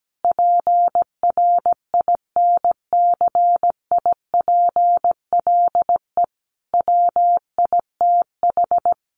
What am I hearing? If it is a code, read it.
Morse code, heard as a keyed tone, about 17 wpm, 700 Hz. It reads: PRINCIPLE WITH
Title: Sentences from Top 1400 Words 17wpm